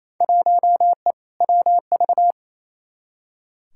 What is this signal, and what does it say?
Morse code, heard as a keyed tone, about 28 wpm, 700 Hz. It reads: 1E WV
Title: Field Day Exchanges Encore - 28wpm